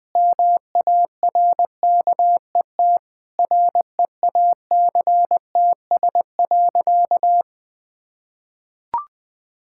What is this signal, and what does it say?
Morse code, heard as a keyed tone, about 20 wpm, 700 Hz. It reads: MARKET REACTS.
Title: Sentences with 8-Word Phrases - 20wpm